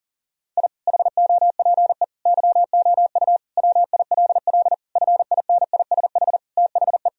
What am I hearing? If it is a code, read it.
Morse code, heard as a keyed tone, about 37 wpm, 700 Hz. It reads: I HOPE YOU WILL FINISH THE
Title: Sentences from Top 1000 Words - Review Part 3 - 40wpm